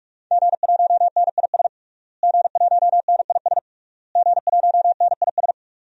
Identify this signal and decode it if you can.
Morse code, heard as a keyed tone, about 46 wpm, 700 Hz. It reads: G1NIS G1NIS G1NIS
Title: DXCC-100 Call Signs - 5 Characters - Repeated 3x - 45wpm